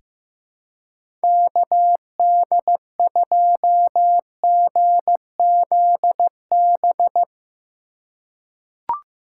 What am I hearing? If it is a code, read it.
Morse code, heard as a keyed tone, about 15 wpm, 700 Hz. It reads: KD2GZB
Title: USA Call Signs - 6 Characters - 4x Character Spacing - 15wpm